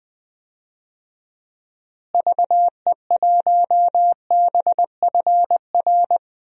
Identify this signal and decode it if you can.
Morse code, heard as a keyed tone, about 20 wpm, 700 Hz. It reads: VE1BFR
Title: DXCC-100 Call Signs - 6 Characters - Repeated 2x - 20wpm